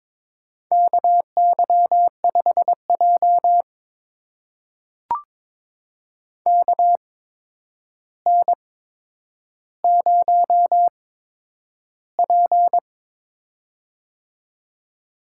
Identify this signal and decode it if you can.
Morse code, heard as a keyed tone, about 22 wpm, spaced as for 9 wpm, 700 Hz. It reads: KY5J KN0P
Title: USA Call Signs - 4 Characters - 8x Character Spacing - 22wpm